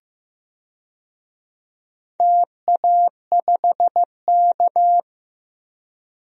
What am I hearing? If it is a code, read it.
Morse code, heard as a keyed tone, about 16 wpm, 700 Hz. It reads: TA5K
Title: DXCC-100 Call Signs - 4 Characters - 15wpm